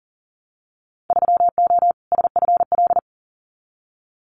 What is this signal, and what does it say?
Morse code, heard as a keyed tone, about 40 wpm, 700 Hz. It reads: 3O SFL
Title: Winter Field Day Exchanges 40wpm